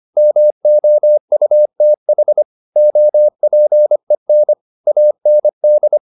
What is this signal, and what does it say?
Morse code, heard as a keyed tone, about 25 wpm, 600 Hz. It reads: MOUTH OPEN AND